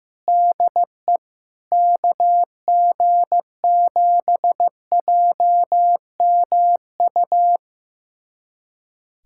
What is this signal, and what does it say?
Morse code, heard as a keyed tone, about 15 wpm, 700 Hz. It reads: DE KG7JMU
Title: USA Callsign DE Callsign - 6 Characters - 15wpm